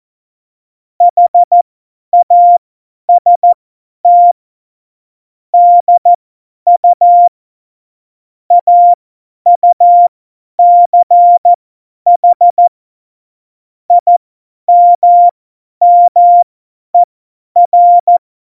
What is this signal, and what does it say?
Morse code, heard as a keyed tone, about 14 wpm, spaced as for 10 wpm, 700 Hz. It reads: HAST DU AUCH IMMER
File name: Morsekurs / Witze / Witz 20. Februar 2018, 50 F70